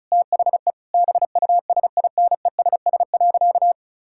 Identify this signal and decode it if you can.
Morse code, heard as a keyed tone, about 34 wpm, 700 Hz. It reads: THE BUSINESS.
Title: Sentences from Top 1000 Words - Review Part 6 - 35wpm